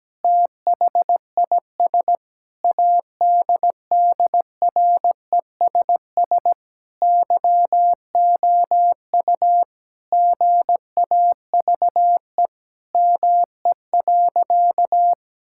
Morse code, heard as a keyed tone, about 17 wpm, 700 Hz. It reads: THIS ADDRESS YOU GAVE ME.
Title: Sentences from Top 1000 Words - Review Part 6 - 17wpm